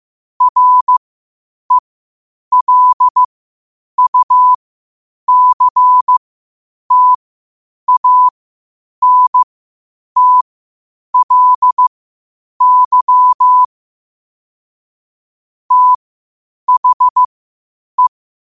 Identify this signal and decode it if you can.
Morse code, heard as a keyed tone, about 15 wpm, spaced as for 8 wpm, 1000 Hz. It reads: RELUCTANTLY THE
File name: qotdfiles / 8wpm